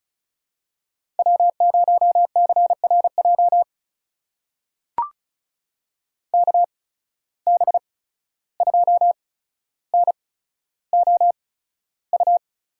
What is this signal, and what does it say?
Morse code, heard as a keyed tone, about 36 wpm, spaced as for 14 wpm, 700 Hz. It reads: W0CRJ KB2NOU
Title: USA Call Signs - 8x Character Spacing - 35wpm